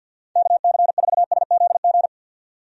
Morse code, heard as a keyed tone, about 50 wpm, 700 Hz. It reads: KC4SZG
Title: USA Call Signs - 6 Characters - 50wpm